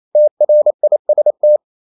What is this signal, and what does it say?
Morse code, heard as a keyed tone, about 28 wpm, 600 Hz. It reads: TRIST
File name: Ferie QRQ 28wpm